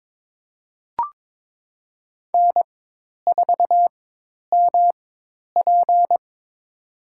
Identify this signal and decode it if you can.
Morse code, heard as a keyed tone, about 21 wpm, spaced as for 10 wpm, 700 Hz. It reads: N4MP
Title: USA Call Signs - 4 Characters - 4x Character Spacing - 22wpm